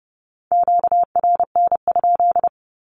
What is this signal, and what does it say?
Morse code, heard as a keyed tone, about 30 wpm, 700 Hz. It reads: QRN?
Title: Q-Codes with Spoken Meaning - 30wpm